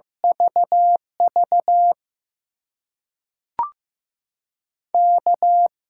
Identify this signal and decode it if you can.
Morse code, heard as a keyed tone, about 15 wpm, spaced as for 5 wpm, 700 Hz. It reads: VVK
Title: USA Call Signs - 8x Character Spacing - 15wpm